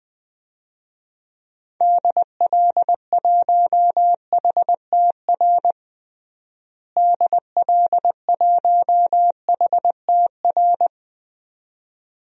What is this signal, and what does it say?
Morse code, heard as a keyed tone, about 20 wpm, 700 Hz. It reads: DL1HTR DL1HTR
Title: DXCC-100 Call Signs - 6 Characters - Repeated 2x - 20wpm